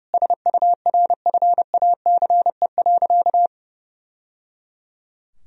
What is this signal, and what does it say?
Morse code, heard as a keyed tone, about 30 wpm, 700 Hz. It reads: SURFACE.
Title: Sentences from Top 1000 Words - Review Part 3 - 30wpm